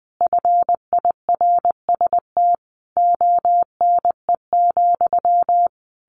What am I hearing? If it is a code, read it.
Morse code, heard as a keyed tone, about 20 wpm, 700 Hz. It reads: FIRST ONE,
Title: A Sound Of Thunder 20wpm